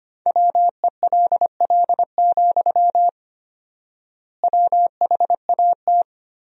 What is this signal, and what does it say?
Morse code, heard as a keyed tone, about 25 wpm, 700 Hz. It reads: WELL, WHAT
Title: 4x Word-Spacing - Sentences from Top 500 Words - 25wpm